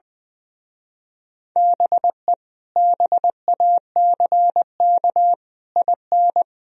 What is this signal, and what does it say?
Morse code, heard as a keyed tone, about 21 wpm, 700 Hz. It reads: BE BACK IN
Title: Speed-Racing - Sets of 5 Words 20wpm